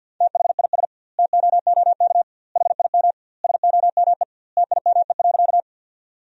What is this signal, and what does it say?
Morse code, heard as a keyed tone, about 50 wpm, 700 Hz. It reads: THIS TOOK HIM SOME TIME.
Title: 2x Word-Spacing - Sentences from Top 300 Words - 50wpm